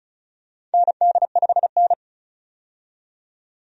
Morse code, heard as a keyed tone, about 34 wpm, 700 Hz. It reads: ND5N
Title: USA Call Signs - 4 Characters - 35wpm